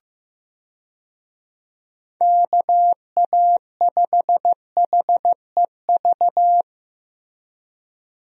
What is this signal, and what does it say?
Morse code, heard as a keyed tone, about 15 wpm, 700 Hz. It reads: KA5HEV
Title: USA Call Signs - 6 Characters - Repeated 3x - 15wpm